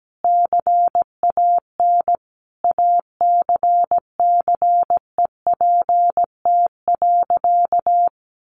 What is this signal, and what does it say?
Morse code, heard as a keyed tone, about 17 wpm, 700 Hz. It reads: CAN ACCEPT.
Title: Sentences from Top 1300 Words 17wpm